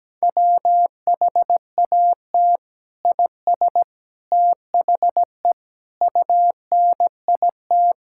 Morse code, heard as a keyed tone, about 17 wpm, 700 Hz. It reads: WHAT IS THE UNIT
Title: Sentences from Top 1000 Words - Review Part 6 - 17wpm